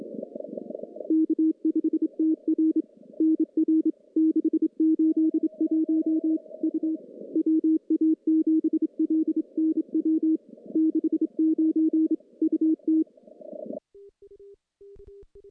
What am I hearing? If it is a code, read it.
Morse code, heard as a keyed tone, about 27 wpm, 320 Hz. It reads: K5TR NR 681 U WA7LNW 69 UT
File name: contests / CW-SS-k5tr- -1300